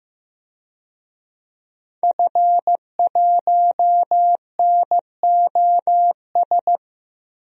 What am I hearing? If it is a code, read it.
Morse code, heard as a keyed tone, about 15 wpm, 700 Hz. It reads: F1NOS
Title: DXCC-100 Call Signs - 5 Characters - 15wpm